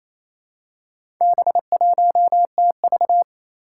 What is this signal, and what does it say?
Morse code, heard as a keyed tone, about 29 wpm, 700 Hz. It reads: B1TV
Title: Four Challenging Letter-Number Combinations 28wpm